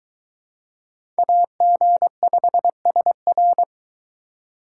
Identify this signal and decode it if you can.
Morse code, heard as a keyed tone, about 24 wpm, 700 Hz. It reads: AG5SR